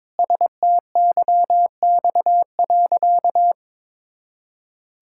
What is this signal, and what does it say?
Morse code, heard as a keyed tone, about 22 wpm, 700 Hz. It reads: STYX.
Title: Pangram Challenge v2 - 22wpm